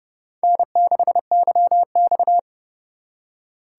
Morse code, heard as a keyed tone, about 36 wpm, 700 Hz. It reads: N6YX
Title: Speed Racing - USA Call Signs 30wpm